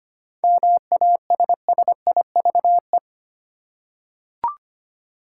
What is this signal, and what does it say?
Morse code, heard as a keyed tone, about 25 wpm, 700 Hz. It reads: MASSIVE
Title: Seven Letter Words - 25wpm